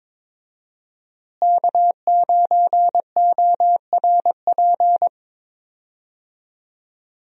Morse code, heard as a keyed tone, about 22 wpm, 700 Hz. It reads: K9ORP
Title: USA Call Signs - 5 Characters - Repeated 2x - 22wpm